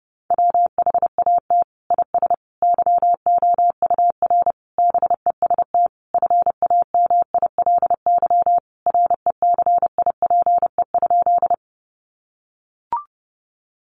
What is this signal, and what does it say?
Morse code, heard as a keyed tone, about 30 wpm, 700 Hz. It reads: WHAT IS YOUR BEST FAMILY RECIPE?
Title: Sentences from Top 1200 Words 30wpm